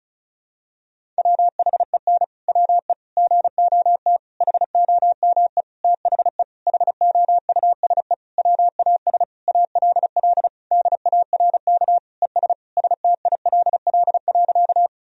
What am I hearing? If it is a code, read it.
Morse code, heard as a keyed tone, about 36 wpm, 700 Hz. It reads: WHEN WE GOT HOME THE HOUSE WAS ALL DARK ES STILL.
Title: Sentences from Top 500 Words and Abbreviations 35wpm